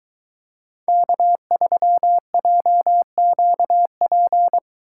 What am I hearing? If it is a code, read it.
Morse code, heard as a keyed tone, about 22 wpm, 700 Hz. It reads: K3JQP